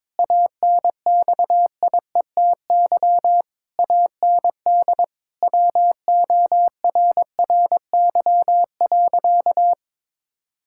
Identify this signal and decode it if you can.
Morse code, heard as a keyed tone, about 22 wpm, 700 Hz. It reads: ANXIETY AND WORRY.
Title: Pangram Challenge v2 - 22wpm